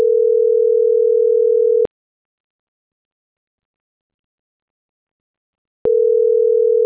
Ringback tone (North America), 440 + 480 Hz.